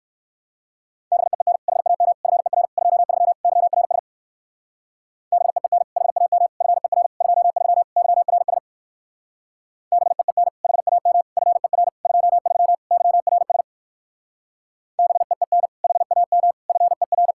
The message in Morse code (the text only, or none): BEEN HAM FER 23 YRS BEEN HAM FER 23 YRS BEEN HAM FER 23 YRS BEEN HAM FER